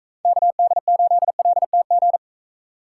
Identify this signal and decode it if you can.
Morse code, heard as a keyed tone, about 40 wpm, 700 Hz. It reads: KD8LTG